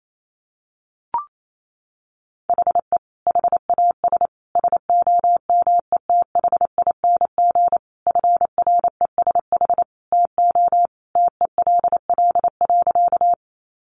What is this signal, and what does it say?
Morse code, heard as a keyed tone, about 28 wpm, 700 Hz. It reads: HE HAS SOMETHING FRESH TO TELL.
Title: Sentences from Top 1200 Words 28wpm